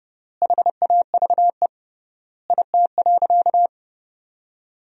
Morse code, heard as a keyed tone, about 30 wpm, 700 Hz. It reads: HAVE IT.
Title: 3x Word-Spacing - Sentences from Top 100 Words - 30wpm